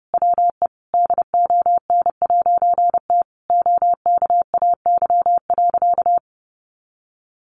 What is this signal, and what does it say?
Morse code, heard as a keyed tone, about 30 wpm, 700 Hz. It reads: WE DON'T OKAY.
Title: A Sound Of Thunder 30wpm